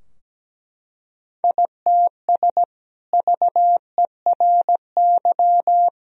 Morse code, heard as a keyed tone, about 17 wpm, 700 Hz. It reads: ITS VERY
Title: Sentences from Top 1000 Words - Review Part 2 - 17wpm